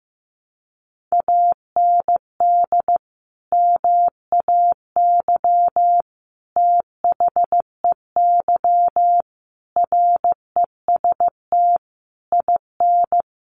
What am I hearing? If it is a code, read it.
Morse code, heard as a keyed tone, about 15 wpm, 700 Hz. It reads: AND MAY THEY REST IN
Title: Sentences from Top 1500 Words 15wpm